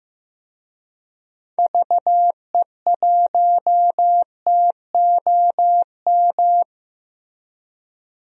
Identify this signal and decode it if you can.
Morse code, heard as a keyed tone, about 15 wpm, 700 Hz. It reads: VE1TOM
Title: DXCC-100 Call Signs - 6 Characters - Repeated 3x - 15wpm